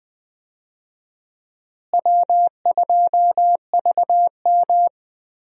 Morse code, heard as a keyed tone, about 20 wpm, 700 Hz. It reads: W2VM
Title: USA Call Signs - 4 Characters - Repeated 3x - 20wpm